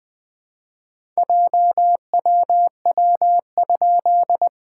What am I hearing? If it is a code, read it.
Morse code, heard as a keyed tone, about 20 wpm, 700 Hz. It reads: JWW?
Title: SOTA QSO Practice 20wpm